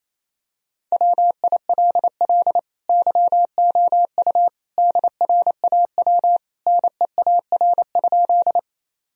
Morse code, heard as a keyed tone, about 28 wpm, 700 Hz. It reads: WILL YOU DRAW NEAR?
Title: Sentences from Top 1000 Words - Review Part 1 - 28wpm